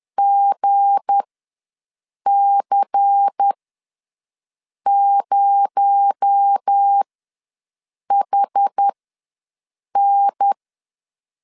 Morse code, heard as a keyed tone, about 10 wpm, spaced as for 6 wpm, 800 Hz. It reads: GC0HN
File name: LEZIONI / 7 GF9Z / 7 G ws 10wpm 3843